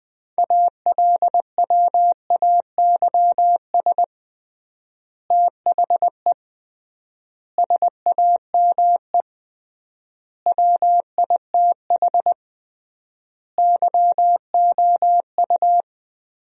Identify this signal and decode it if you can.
Morse code, heard as a keyed tone, about 20 wpm, 700 Hz. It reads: ALWAYS THE SAME WITH YOU
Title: 3x Word-Spacing - Sentences from Top 300 Words - 20wpm